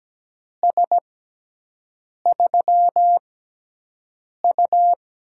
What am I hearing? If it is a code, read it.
Morse code, heard as a keyed tone, about 17 wpm, spaced as for 6 wpm, 700 Hz. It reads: S3U
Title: USA Call Signs - 4 Characters - 6x Character Spacing - 17wpm